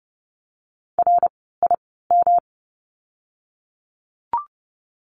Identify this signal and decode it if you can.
Morse code, heard as a keyed tone, about 30 wpm, spaced as for 17 wpm, 700 Hz. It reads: RIM
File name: course / 30:10wpm / 108 Lesson 18f - Letter m - Words 30wpm